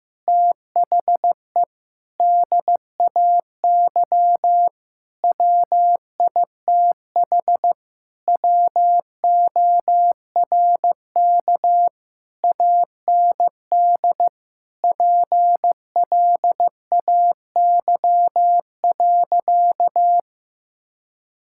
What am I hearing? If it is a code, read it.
Morse code, heard as a keyed tone, about 14 wpm, 700 Hz. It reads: THE DAY WITH WORK AND PLAY.
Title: Sentences from Top 1000 Words - Review Part 6 - 15wpm